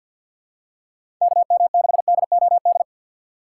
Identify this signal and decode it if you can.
Morse code, heard as a keyed tone, about 52 wpm, 700 Hz. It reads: KM6DOD
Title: USA Call Signs - 6 Characters - 6x Character Spacing - 50wpm